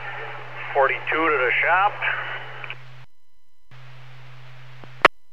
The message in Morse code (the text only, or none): none